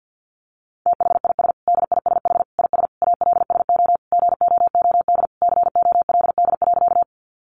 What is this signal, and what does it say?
Morse code, heard as a keyed tone, about 50 wpm, 700 Hz. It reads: THIS DISH IS ALSO GOOD COLD.
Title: Sentences from Top 1300 Words 50wpm